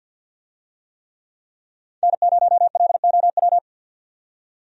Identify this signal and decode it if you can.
Morse code, heard as a keyed tone, about 48 wpm, 700 Hz. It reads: N0LOW
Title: USA Call Signs - 5 Characters - 50wpm